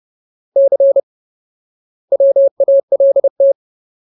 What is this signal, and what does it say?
Morse code, heard as a keyed tone, about 30 wpm, 550 Hz. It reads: C WALT
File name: Ultimate Word List / 30WPM Ultimate